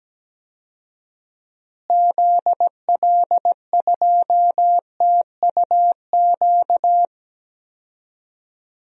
Morse code, heard as a keyed tone, about 17 wpm, 700 Hz. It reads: ZL2TUQ
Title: DXCC-100 Call Signs - 6 Characters - Repeated 2x - 17wpm